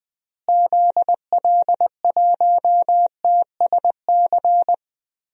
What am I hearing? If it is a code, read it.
Morse code, heard as a keyed tone, about 21 wpm, 700 Hz. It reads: ZL1TSC
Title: DXCC-100 Call Signs - 6 Characters - Repeated 3x - 20wpm